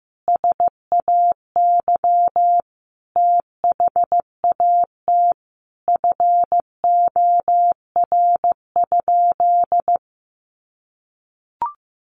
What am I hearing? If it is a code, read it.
Morse code, heard as a keyed tone, about 15 wpm, 700 Hz. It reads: SAY THAT FOR?
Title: Sentences from Top 200 Words 15wpm